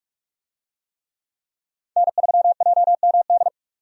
Morse code, heard as a keyed tone, about 46 wpm, 700 Hz. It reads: N3JMD
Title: USA Call Signs - 5 Characters - 45wpm